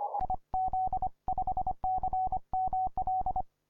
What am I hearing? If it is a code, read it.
Morse code, heard as a keyed tone, about 30 wpm, 740 Hz. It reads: IZ5CML
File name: simple-file-list / k5d 20m cw